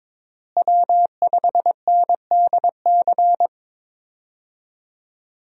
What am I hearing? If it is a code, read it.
Morse code, heard as a keyed tone, about 23 wpm, 700 Hz. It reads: W5NDC
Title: USA Call Signs - 5 Characters - Repeated 3x - 22wpm